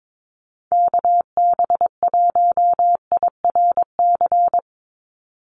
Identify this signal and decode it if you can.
Morse code, heard as a keyed tone, about 22 wpm, 700 Hz. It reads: KB1IRC
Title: USA Call Signs 22wpm